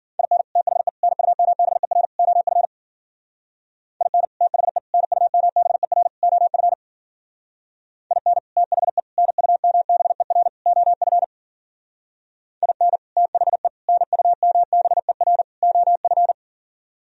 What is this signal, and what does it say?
Morse code, heard as a keyed tone, about 50 wpm, 700 Hz. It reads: IN THE NUMBER OF IN THE NUMBER OF IN THE NUMBER OF IN THE NUMBER OF